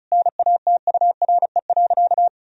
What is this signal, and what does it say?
Morse code, heard as a keyed tone, about 36 wpm, 700 Hz. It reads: NATURE.
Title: Sentences from Top 1000 Words - Review Part 4 - 35wpm